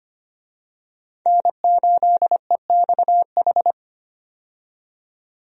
Morse code, heard as a keyed tone, about 25 wpm, 700 Hz. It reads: N8EXH
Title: USA Call Signs - 5 Characters - 25wpm